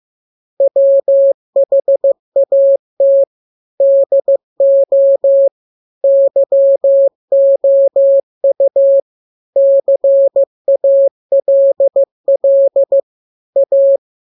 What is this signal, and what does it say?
Morse code, heard as a keyed tone, about 15 wpm, 550 Hz. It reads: WHAT DO YOU CALL A